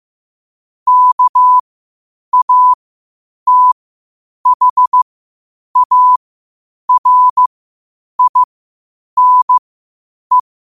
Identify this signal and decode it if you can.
Morse code, heard as a keyed tone, about 15 wpm, spaced as for 8 wpm, 1000 Hz. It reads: KATHARINE
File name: qotdfiles / 8wpm